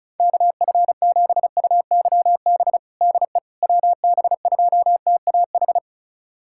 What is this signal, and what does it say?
Morse code, heard as a keyed tone, about 36 wpm, 690 Hz. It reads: KF7UYB DE WB2TAH
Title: USA Callsign DE Callsign - 6 Characters - 35wpm